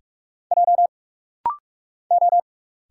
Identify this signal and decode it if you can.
Morse code, heard as a keyed tone, about 44 wpm, 700 Hz. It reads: J O
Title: Single Letter-Number Alt - Rapid-Fire 45wpm